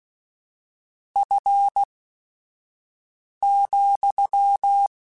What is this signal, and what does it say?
Morse code, heard as a keyed tone, about 16 wpm, spaced as for 5 wpm, 800 Hz. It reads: F,
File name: Weather Sets / Weather Sets-mp3s / Weather 4 14 @5